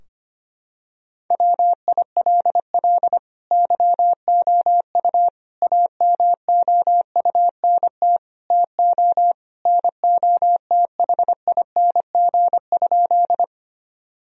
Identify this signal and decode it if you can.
Morse code, heard as a keyed tone, about 25 wpm, 700 Hz. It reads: WILL YOU AMOUNT TO NOTHING?
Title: Sentences from Top 1000 Words - Review Part 4 - 25wpm